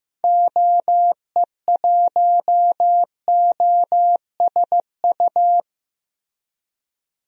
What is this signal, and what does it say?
Morse code, heard as a keyed tone, about 15 wpm, 700 Hz. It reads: OE1OSU
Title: DXCC-100 Call Signs - 6 Characters - Repeated 2x - 15wpm